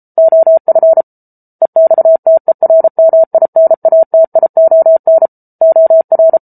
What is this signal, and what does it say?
Morse code, heard as a keyed tone, about 34 wpm, 650 Hz. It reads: OF EXTERMINATION OR